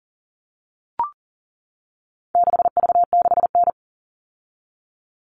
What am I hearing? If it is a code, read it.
Morse code, heard as a keyed tone, about 39 wpm, 700 Hz. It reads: 6V6N
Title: All Cut Numbers - 4-Digit 40wpm